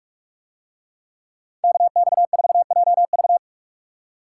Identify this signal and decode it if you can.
Morse code, heard as a keyed tone, about 41 wpm, 700 Hz. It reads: KX4JV